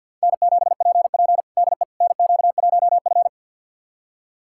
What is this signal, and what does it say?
Morse code, heard as a keyed tone, about 47 wpm, 700 Hz. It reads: N7PP DE NQ1F
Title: USA Callsign DE Callsign - 4 Characters - 50wpm